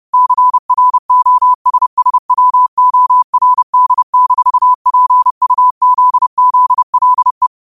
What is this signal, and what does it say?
Morse code, heard as a keyed tone, about 30 wpm, 1000 Hz. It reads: CROSSWORD-PUZZLE
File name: qotdfiles / 30wpm